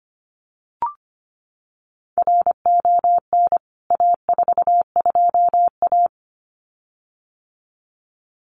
Morse code, heard as a keyed tone, about 25 wpm, 700 Hz. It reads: RON A42A
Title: CWT Exchange 25wpm